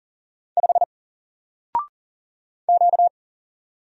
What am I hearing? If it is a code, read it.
Morse code, heard as a keyed tone, about 40 wpm, 700 Hz. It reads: 5 Q
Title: Single Letter-Number 40wpm